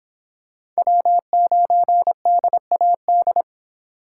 Morse code, heard as a keyed tone, about 25 wpm, 700 Hz. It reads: W9DAD